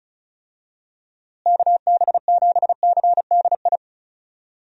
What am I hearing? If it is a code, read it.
Morse code, heard as a keyed tone, about 36 wpm, 700 Hz. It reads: KB7CDI
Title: USA Call Signs - 6 Characters - Repeated 3x - 35wpm